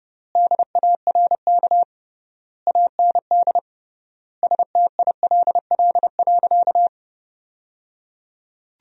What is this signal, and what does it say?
Morse code, heard as a keyed tone, about 32 wpm, 700 Hz. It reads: DARK AND STILL.
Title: 3x Word-Spacing - Sentences from Top 500 Words - 30wpm